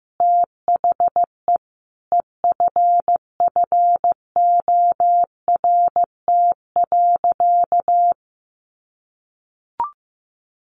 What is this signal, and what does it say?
Morse code, heard as a keyed tone, about 15 wpm, 700 Hz. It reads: THE EFFORT.
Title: Sentences from Top 1200 Words 15wpm